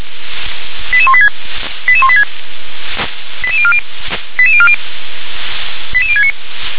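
Five-tone selective call (ZVEI), heard as quick runs of five tones, tones of about 70 ms. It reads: 88186 88186 88040 88040 88069
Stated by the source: ZVEI five-tone call 88103 received by SDR.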